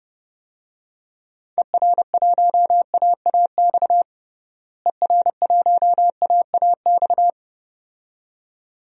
Morse code, heard as a keyed tone, about 30 wpm, 700 Hz. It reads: ER1AAX ER1AAX
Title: DXCC-100 Call Signs - 6 Characters - Repeated 2x - 30wpm